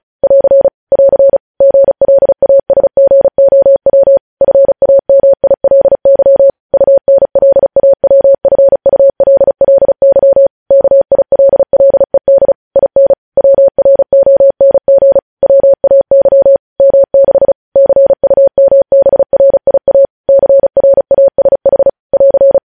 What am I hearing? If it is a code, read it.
Morse code, heard as a keyed tone, about 36 wpm, 550 Hz. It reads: <AR> <AR> GLASGOW FAMILY UNLAWFULLY KILLED IN WRONG WAY M6 CUMBRIA CRASH <AR>
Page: news headlines in morse code at 35 wpm